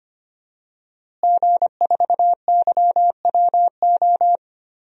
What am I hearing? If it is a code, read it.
Morse code, heard as a keyed tone, about 25 wpm, 700 Hz. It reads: G4YWO
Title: DXCC-100 Call Signs - 5 Characters - Repeated 3x - 25wpm